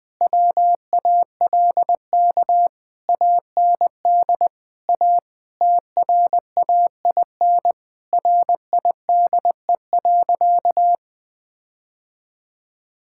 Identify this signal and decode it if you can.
Morse code, heard as a keyed tone, about 20 wpm, 700 Hz. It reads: WALK AND A TRAIN RIDE.
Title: Difficult Sentences with Hints - 20wpm